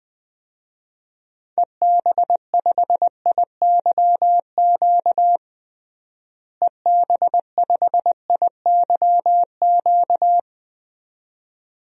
Morse code, heard as a keyed tone, about 20 wpm, 690 Hz. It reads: EB5IYQ EB5IYQ
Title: DXCC-100 Call Signs - 6 Characters - Repeated 2x - 20wpm